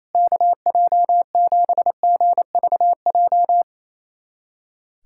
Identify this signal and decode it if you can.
Morse code, heard as a keyed tone, about 28 wpm, 700 Hz. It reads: KJ7GVJ
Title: USA Callsign DE Callsign 28wpm